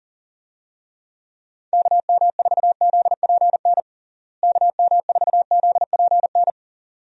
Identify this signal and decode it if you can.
Morse code, heard as a keyed tone, about 40 wpm, 700 Hz. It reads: KM4ZPN KM4ZPN
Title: USA Call Signs - 6 Characters - Repeated 2x - 40wpm